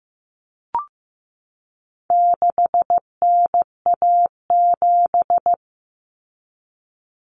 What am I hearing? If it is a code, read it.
Morse code, heard as a keyed tone, about 15 wpm, 700 Hz. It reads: 6NA7
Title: Common Cut Numbers - 4-Digit 15wpm